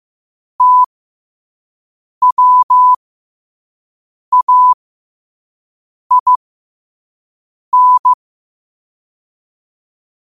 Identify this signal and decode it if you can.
Morse code, heard as a keyed tone, about 15 wpm, spaced as for 5 wpm, 1000 Hz. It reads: TWAIN
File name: qotdfiles / 5wpm